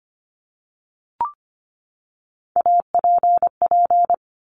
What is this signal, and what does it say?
Morse code, heard as a keyed tone, about 25 wpm, 700 Hz. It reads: APP
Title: Top Three Letter Combinations in Words 25wpm